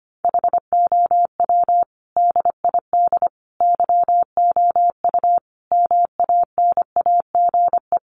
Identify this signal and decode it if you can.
Morse code, heard as a keyed tone, about 25 wpm, 700 Hz. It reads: HOW DID YOU MANAGE
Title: Sentences from Top 1400 Words 25wpm